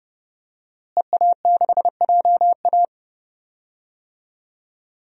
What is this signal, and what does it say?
Morse code, heard as a keyed tone, about 30 wpm, 700 Hz. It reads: EA6JA
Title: DXCC-100 Call Signs - 5 Characters - Repeated 2x - 30wpm